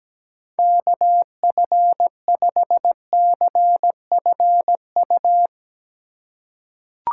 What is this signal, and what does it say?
Morse code, heard as a keyed tone, about 17 wpm, 700 Hz. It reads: KF5CFU
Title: USA Call Signs - 6 Characters - 4x Character Spacing - 17wpm